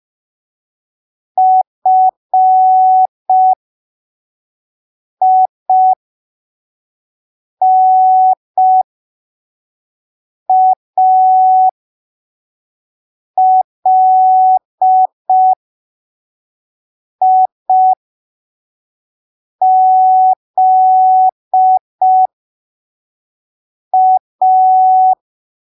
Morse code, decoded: FINALIZA